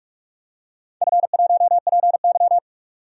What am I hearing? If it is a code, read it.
Morse code, heard as a keyed tone, about 44 wpm, 700 Hz. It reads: F1PY
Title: DXCC-100 Call Signs - 4 Characters - 45wpm